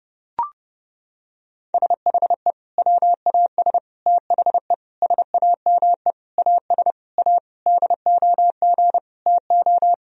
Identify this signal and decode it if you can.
Morse code, heard as a keyed tone, about 32 wpm, 700 Hz. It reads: SHE WAS THE SAME AS A DOG TO
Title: Sentences from Top 1000 Words - Review Part 2 - 30wpm